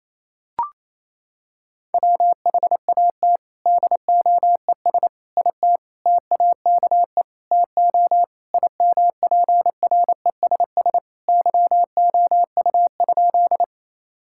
Morse code, heard as a keyed tone, about 28 wpm, 700 Hz. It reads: WHAT DOES IT TAKE TO IMPRESS YOU?
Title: Sentences from Top 1600 Words 28wpm